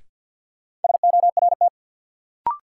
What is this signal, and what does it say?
Morse code, heard as a keyed tone, about 50 wpm, 700 Hz. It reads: SORT
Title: Sets of 2 Words - New Word Drill - 50wpm